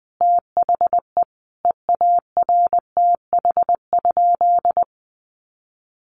Morse code, heard as a keyed tone, about 19 wpm, 700 Hz. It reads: THE EARTH?
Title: Sentences from Top 1200 Words 20wpm